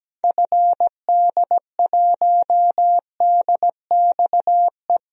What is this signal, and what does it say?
Morse code, heard as a keyed tone, about 17 wpm, 700 Hz. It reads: FD1DXE
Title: DXCC-100 Call Signs - 6 Characters - Repeated 3x - 17wpm